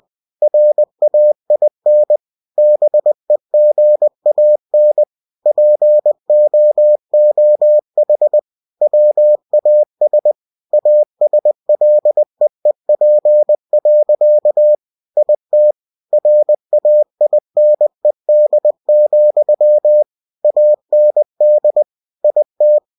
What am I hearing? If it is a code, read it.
Morse code, heard as a keyed tone, about 20 wpm, 600 Hz. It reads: RAIN BEGAN POOH WAS ASLEEP. IT RAINED, AND IT
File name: cw-books / winnie-the-pooh / winnie-20wpm-page-0093